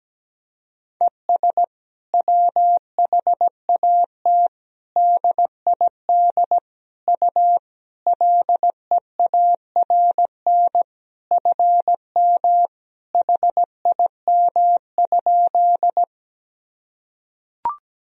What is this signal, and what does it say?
Morse code, heard as a keyed tone, about 17 wpm, 700 Hz. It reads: ES WHAT DID U LEARN FM HIM?
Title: Sentences from Top 300 Words and Abbreviations 17wpm